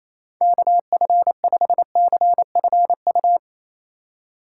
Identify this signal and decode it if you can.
Morse code, heard as a keyed tone, about 28 wpm, 700 Hz. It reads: KF5CFU
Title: USA Call Signs - 6 Characters - 4x Character Spacing - 28wpm